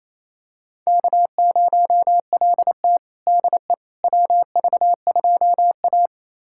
Morse code, heard as a keyed tone, about 27 wpm, 690 Hz. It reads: K0LT DE WV2A
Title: USA Callsign DE Callsign - 4 Characters - 28wpm